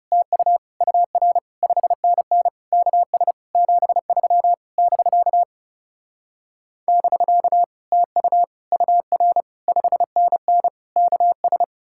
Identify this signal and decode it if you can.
Morse code, heard as a keyed tone, about 40 wpm, 700 Hz. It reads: TU UR 5NN KS 73 <BK> <BK> TU UR 5NN KS